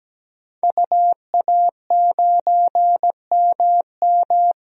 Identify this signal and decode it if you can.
Morse code, heard as a keyed tone, about 17 wpm, 700 Hz. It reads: UA9MM
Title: DXCC-100 Call Signs - 5 Characters - 17wpm